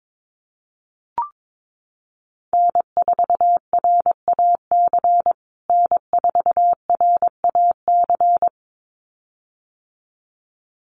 Morse code, heard as a keyed tone, about 22 wpm, 700 Hz. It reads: N4RAC N4RAC
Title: USA Call Signs Repeated 2x 22wpm